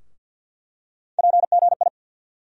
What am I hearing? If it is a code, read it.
Morse code, heard as a keyed tone, about 50 wpm, 700 Hz. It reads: PGI
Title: USA Call Sign Postfixes - 3 Letters 50wpm